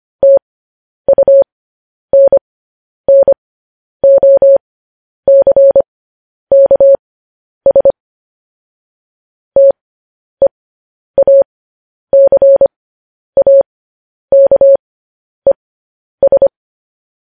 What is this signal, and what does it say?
Morse code, heard as a keyed tone, about 25 wpm, spaced as for 10 wpm, 550 Hz. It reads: TUNNOCKS TEACAKES
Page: news headlines in morse code at 10 wpm